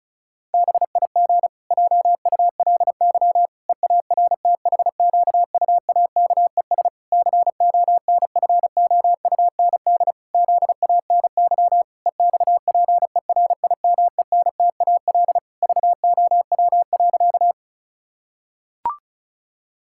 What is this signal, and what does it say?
Morse code, heard as a keyed tone, about 34 wpm, 700 Hz. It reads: BIG JULY EARTHQUAKES CONFOUND ZANY EXPERIMENTAL VOW.
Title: Pangram Challenge v2 - 35wpm